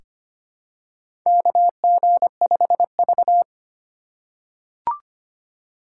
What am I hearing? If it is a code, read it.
Morse code, heard as a keyed tone, about 25 wpm, 700 Hz. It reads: KG5V
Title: K1USN SST QSO Practice - 25wpm